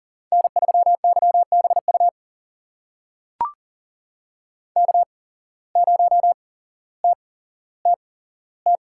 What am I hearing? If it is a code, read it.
Morse code, heard as a keyed tone, about 40 wpm, spaced as for 17 wpm, 700 Hz. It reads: N3YBU K0TTT
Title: USA Call Signs - 5 Characters - 8x Character Spacing - 40wpm